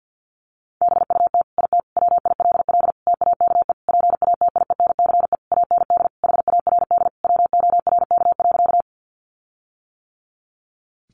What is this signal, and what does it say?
Morse code, heard as a keyed tone, about 49 wpm, 700 Hz. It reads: BUT IT WILL TAKE PATIENCE AND HARD WORK.
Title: Sentences from Top 1300 Words 50wpm